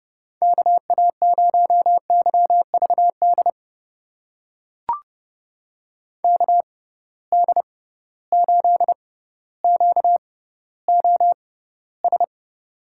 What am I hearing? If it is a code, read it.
Morse code, heard as a keyed tone, about 30 wpm, spaced as for 15 wpm, 700 Hz. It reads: KA0YVD KD8QOS
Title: USA Call Signs - 6 Characters - 6x Character Spacing - 30wpm